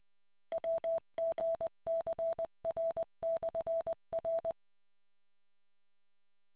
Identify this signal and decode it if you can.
Morse code, heard as a keyed tone, about 20 wpm, 660 Hz. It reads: WGCR/R